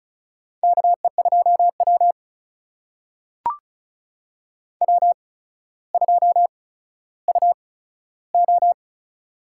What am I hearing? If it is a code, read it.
Morse code, heard as a keyed tone, about 34 wpm, spaced as for 15 wpm, 700 Hz. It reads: KE2W W2UO
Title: USA Call Signs - 4 Characters - 8x Character Spacing - 35wpm